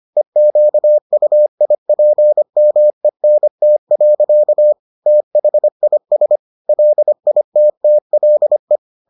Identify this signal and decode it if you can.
Morse code, heard as a keyed tone, about 25 wpm, 600 Hz. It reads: EQUIPMENT. THIS LITTLE